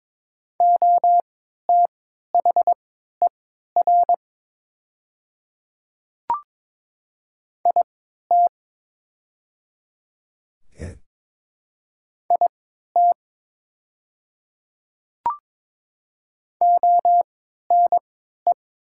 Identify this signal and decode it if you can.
Morse code, heard as a keyed tone, about 22 wpm, spaced as for 13 wpm, 700 Hz. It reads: OTHER IT IT ONE
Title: 3x Character Spacing - Top 100 Words - 22wpm